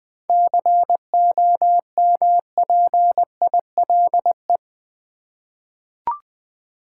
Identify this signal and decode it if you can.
Morse code, heard as a keyed tone, about 20 wpm, 700 Hz. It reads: COMPILE
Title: 7 Letter Words - First 3 Letters Given - 20wpm